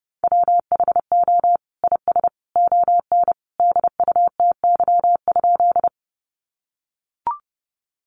Morse code, heard as a keyed tone, about 30 wpm, 700 Hz. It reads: WHO IS ON DUTY?
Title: Sentences from Top 1500 Words 30wpm